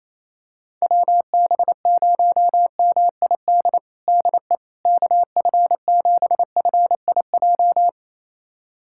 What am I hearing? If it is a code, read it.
Morse code, heard as a keyed tone, about 28 wpm, 700 Hz. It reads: WB0MID DE KF7FIJ
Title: USA Callsign DE Callsign - 6 Characters - 28wpm